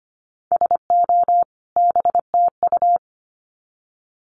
Morse code, heard as a keyed tone, about 25 wpm, 700 Hz. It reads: SO BTU
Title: Common QSO Phrases 25wpm